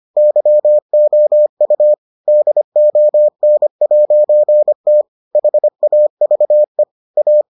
Morse code, heard as a keyed tone, about 25 wpm, 600 Hz. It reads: YOU DON'T HAVE A